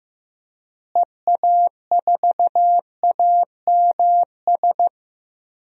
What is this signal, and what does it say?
Morse code, heard as a keyed tone, about 15 wpm, 700 Hz. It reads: EA4AMS
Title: DXCC-100 Call Signs - 6 Characters - Repeated 3x - 15wpm